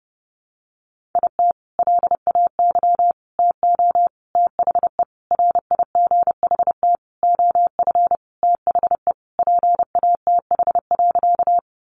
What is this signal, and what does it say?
Morse code, heard as a keyed tone, about 30 wpm, 700 Hz. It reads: IT LAY TO THE RIGHT OF THE PATH.
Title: Sentences from Top 1400 Words 30wpm